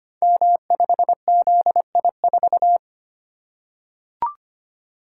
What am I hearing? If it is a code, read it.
Morse code, heard as a keyed tone, about 25 wpm, 700 Hz. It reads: M5ZI4
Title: Code Groups - 5 Letter:Numbers - 25wpm